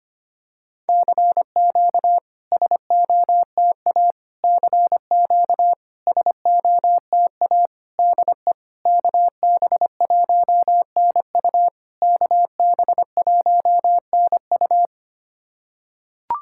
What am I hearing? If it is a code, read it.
Morse code, heard as a keyed tone, about 25 wpm, 690 Hz. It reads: CQ SOTA CQ SOTA DE KB1NU KB1NU
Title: SOTA QSO Practice 25wpm